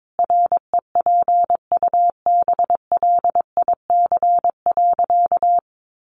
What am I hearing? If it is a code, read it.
Morse code, heard as a keyed tone, about 22 wpm, 700 Hz. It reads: REPUBLIC.
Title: Sentences from Top 1500 Words 22wpm